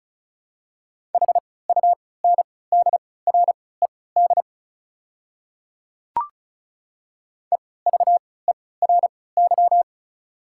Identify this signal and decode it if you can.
Morse code, heard as a keyed tone, about 34 wpm, spaced as for 20 wpm, 690 Hz. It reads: HUNDRED EVERY
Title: 3x Character Spacing - Top 500 Words - 35wpm